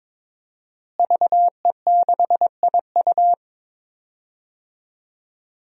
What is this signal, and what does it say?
Morse code, heard as a keyed tone, about 22 wpm, 700 Hz. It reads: VE6IU
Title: DXCC-100 Call Signs - 5 Characters - Repeated 3x - 22wpm